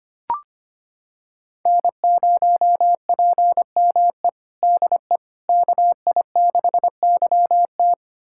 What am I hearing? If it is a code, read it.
Morse code, heard as a keyed tone, about 25 wpm, 700 Hz. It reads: N0PME DE KI6YT
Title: USA Callsign DE Callsign - 5 Characters - 25wpm